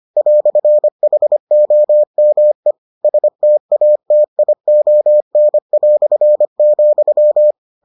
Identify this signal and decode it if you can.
Morse code, heard as a keyed tone, about 25 wpm, 600 Hz. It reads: "HOME STATION",